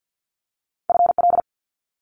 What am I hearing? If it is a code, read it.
Morse code, heard as a keyed tone, about 50 wpm, 700 Hz. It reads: FL
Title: Multi-letter Phonemes Quick Fire 50wpm